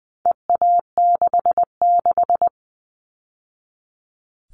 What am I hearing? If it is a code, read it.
Morse code, heard as a keyed tone, about 20 wpm, 700 Hz. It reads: EA66
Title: All Cut Numbers - 4-Digit 20wpm